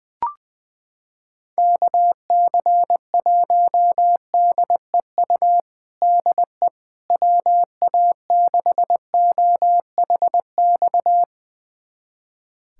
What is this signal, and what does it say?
Morse code, heard as a keyed tone, about 19 wpm, 700 Hz. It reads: KC1DEU DE WA6OHX
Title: USA Callsign DE Callsign - 6 Characters - 20wpm